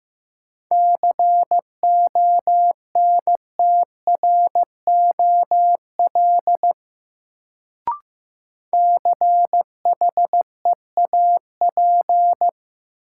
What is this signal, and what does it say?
Morse code, heard as a keyed tone, about 15 wpm, 700 Hz. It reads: CONTROL CHEAP
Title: Sentences from Top 1200 Words - New Word Drill Alt - 15wpm